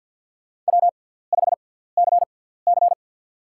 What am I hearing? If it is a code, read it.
Morse code, heard as a keyed tone, about 52 wpm, 700 Hz. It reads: W 5 C C
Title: USA Call Signs - 6x Character Spacing - 50wpm